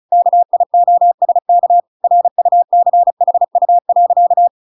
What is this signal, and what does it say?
Morse code, heard as a keyed tone, about 34 wpm, 700 Hz. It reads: KIOSK RUCHU.